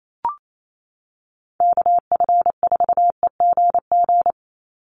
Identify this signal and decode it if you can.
Morse code, heard as a keyed tone, about 28 wpm, 700 Hz. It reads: KF4EGG
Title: USA Call Signs 28wpm